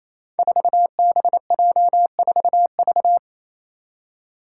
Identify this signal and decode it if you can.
Morse code, heard as a keyed tone, about 27 wpm, 700 Hz. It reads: 4BJ4V
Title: Five Challenging Letter-Number Combinations 28wpm